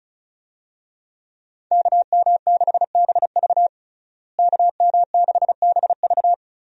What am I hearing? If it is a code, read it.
Morse code, heard as a keyed tone, about 36 wpm, 700 Hz. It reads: KM6BV KM6BV
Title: USA Call Signs - 5 Characters - Repeated 2x - 35wpm